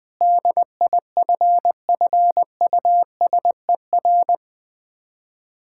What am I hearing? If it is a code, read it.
Morse code, heard as a keyed tone, about 20 wpm, 690 Hz. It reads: DIFFUSER
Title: Difficult Words with Hints - 20wpm